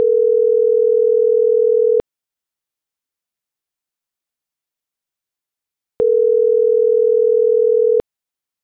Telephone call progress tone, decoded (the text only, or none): ringback tone